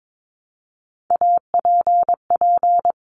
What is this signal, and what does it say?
Morse code, heard as a keyed tone, about 22 wpm, 700 Hz. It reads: APP
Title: Top Three Letter Combinations in Words 22wpm